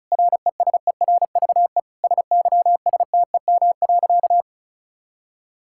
Morse code, heard as a keyed tone, about 36 wpm, 700 Hz. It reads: RESERVE SYSTEM.
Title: Sentences from Top 1000 Words - Review Part 6 - 35wpm